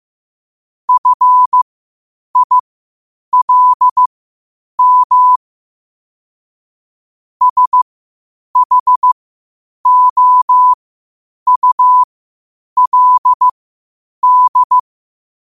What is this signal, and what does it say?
Morse code, heard as a keyed tone, about 15 wpm, spaced as for 8 wpm, 1000 Hz. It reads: FILM SHOULD
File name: qotdfiles / 8wpm